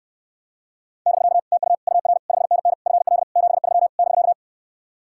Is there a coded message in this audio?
Yes: Morse code from 1.1 to 4.3 s, heard as a keyed tone, about 69 wpm, 700 Hz. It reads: <BK> TU UR 5NN FL 73 <BK>